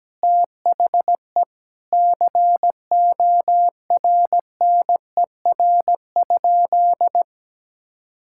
Morse code, heard as a keyed tone, about 17 wpm, 700 Hz. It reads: THE CORNER?
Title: Sentences from Top 1600 Words 17wpm